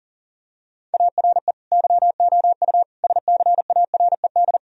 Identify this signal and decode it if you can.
Morse code, heard as a keyed tone, about 37 wpm, 700 Hz. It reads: ARE YOU SCARED
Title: Sentences from Top 1000 Words - Review Part 4 - 40wpm